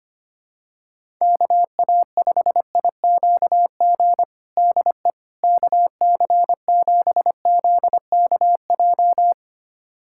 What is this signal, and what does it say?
Morse code, heard as a keyed tone, about 25 wpm, 700 Hz. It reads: KA5IQG DE KC7ZKJ
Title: USA Callsign DE Callsign - 6 Characters - 25wpm